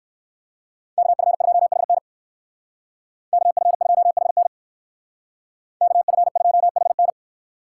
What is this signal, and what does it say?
Morse code, heard as a keyed tone, about 58 wpm, 700 Hz. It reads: KF2HN KF2HN KF2HN